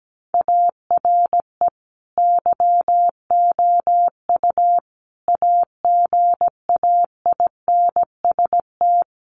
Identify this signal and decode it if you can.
Morse code, heard as a keyed tone, about 17 wpm, 700 Hz. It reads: ARE YOU AGAINST
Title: Sentences from Top 1400 Words 17wpm